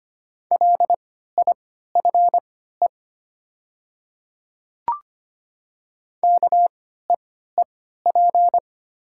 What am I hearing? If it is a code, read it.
Morse code, heard as a keyed tone, about 25 wpm, spaced as for 14 wpm, 700 Hz. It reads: LIFE KEEP
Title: 3x Character Spacing - Top 300 Words - 25wpm